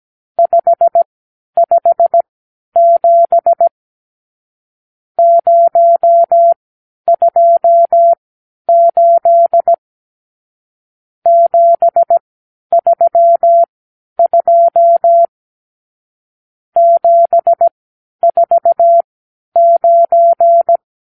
Morse code, heard as a keyed tone, about 17 wpm, spaced as for 10 wpm, 680 Hz. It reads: 557 028 732 749
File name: arrl / Review (1-0)